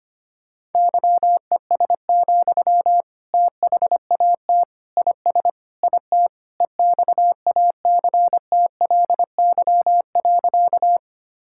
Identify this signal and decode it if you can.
Morse code, heard as a keyed tone, about 25 wpm, 700 Hz. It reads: YES, THAT IS IT EXACTLY.
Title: Sentences from Top 1000 Words - Review Part 6 - 25wpm